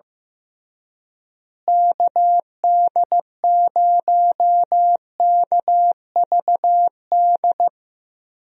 Morse code, heard as a keyed tone, about 15 wpm, 700 Hz. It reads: KD0KVD
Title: USA Call Signs - 6 Characters - Repeated 2x - 15wpm